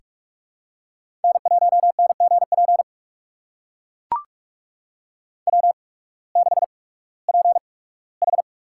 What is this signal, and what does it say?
Morse code, heard as a keyed tone, about 42 wpm, spaced as for 21 wpm, 700 Hz. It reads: N1NGP W6PH
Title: USA Call Signs - 8x Character Spacing - 45wpm